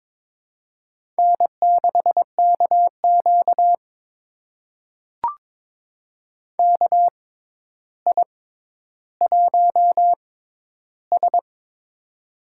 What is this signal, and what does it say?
Morse code, heard as a keyed tone, about 22 wpm, spaced as for 11 wpm, 700 Hz. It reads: N6KQ KI1S
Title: USA Call Signs - 4 Characters - 6x Character Spacing - 22wpm